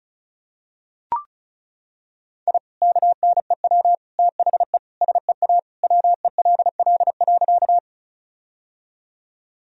Morse code, heard as a keyed tone, about 36 wpm, 700 Hz. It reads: I KNEW THE SEA WELL.
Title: Sentences from Top 1000 Words - Review Part 1 - 35wpm